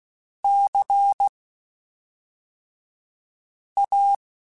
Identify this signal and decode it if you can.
Morse code, heard as a keyed tone, about 16 wpm, spaced as for 3 wpm, 800 Hz. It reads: CA